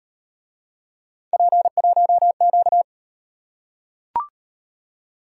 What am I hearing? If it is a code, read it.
Morse code, heard as a keyed tone, about 36 wpm, 700 Hz. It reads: P1Q